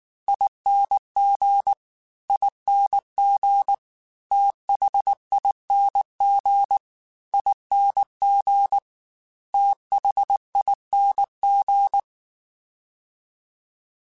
Morse code, heard as a keyed tone, about 19 wpm, 800 Hz. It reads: ING ING THING ING THING
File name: Special INGs / Special INGs-mp3s / Slow ing @18